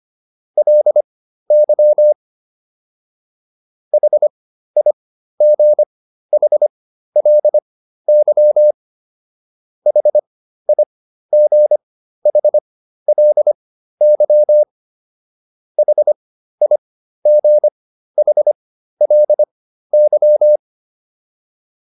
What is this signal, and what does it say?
Morse code, heard as a keyed tone, about 25 wpm, spaced as for 13 wpm, 600 Hz. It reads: LY HIGHLY HIGHLY HIGHLY